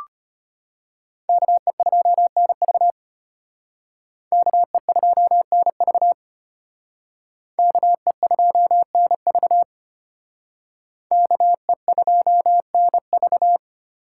Morse code, heard as a keyed tone, about 32 wpm, 700 Hz. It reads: KE2NV KE2NV KE2NV KE2NV